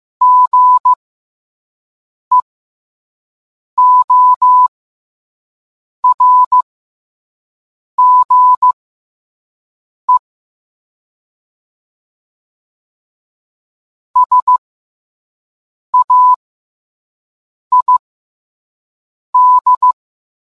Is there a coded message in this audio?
Yes: Morse code from 0.2 to 19.9 s, heard as a keyed tone, about 15 wpm, spaced as for 5 wpm, 1000 Hz. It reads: GEORGE SAID